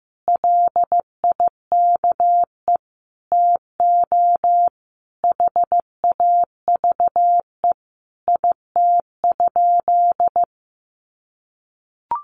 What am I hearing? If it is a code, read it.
Morse code, heard as a keyed tone, about 15 wpm, 700 Hz. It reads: LIKE TO HAVE IT?
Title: Sentences from Top 100 Words 15wpm